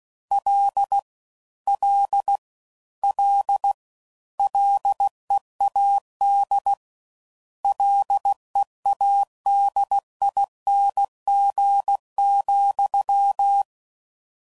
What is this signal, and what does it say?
Morse code, heard as a keyed tone, about 16 wpm, 800 Hz. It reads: L L L LEAD LEADING,